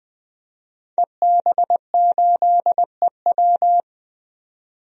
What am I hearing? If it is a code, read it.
Morse code, heard as a keyed tone, about 20 wpm, 700 Hz. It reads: EB8EW
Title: DXCC-100 Call Signs - 5 Characters - 20wpm